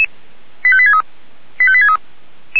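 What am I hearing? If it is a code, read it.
Five-tone selective call (ZVEI), heard as quick runs of five tones, tones of about 70 ms. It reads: 85863 85863
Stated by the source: ZVEI five-tone selective call received on an SDR.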